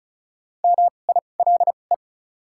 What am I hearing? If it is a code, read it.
Morse code, heard as a keyed tone, about 34 wpm, spaced as for 25 wpm, 700 Hz. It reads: MILE
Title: 2x Character Spacing - Top 500 Words - 35wpm